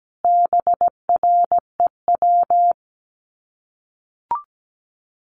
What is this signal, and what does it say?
Morse code, heard as a keyed tone, about 17 wpm, 700 Hz. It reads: BREW
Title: Common QSO Phrases 17wpm